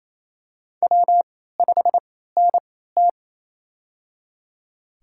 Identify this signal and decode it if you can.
Morse code, heard as a keyed tone, about 27 wpm, spaced as for 16 wpm, 700 Hz. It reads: W5NT
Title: USA Call Signs - 4 Characters - 3x Character Spacing - 28wpm